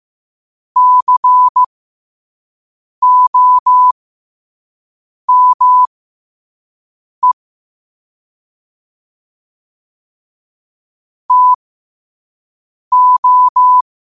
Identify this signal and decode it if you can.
Morse code, heard as a keyed tone, about 15 wpm, spaced as for 5 wpm, 1000 Hz. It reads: COME TO
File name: qotdfiles / 5wpm